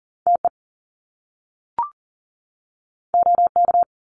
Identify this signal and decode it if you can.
Morse code, heard as a keyed tone, about 38 wpm, 700 Hz. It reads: TE OK
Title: Top Two Letter Combinations in Words 40wpm